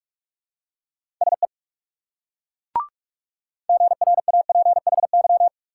Morse code, heard as a keyed tone, about 38 wpm, 700 Hz. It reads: SE GRAPHY
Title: Speed-Racing Multi-letter Phonemes - 30wpm